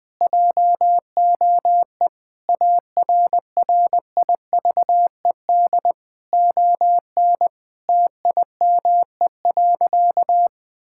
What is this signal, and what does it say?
Morse code, heard as a keyed tone, about 21 wpm, 700 Hz. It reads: JOE ARRIVED ON TIME.
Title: Sentences with 4-Word Phrases - Spoken Beginning - 20wpm